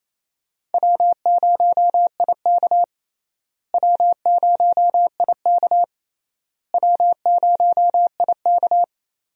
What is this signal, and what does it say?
Morse code, heard as a keyed tone, about 28 wpm, 700 Hz. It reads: W0IK W0IK W0IK
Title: USA Call Signs - 4 Characters - Repeated 3x - 28wpm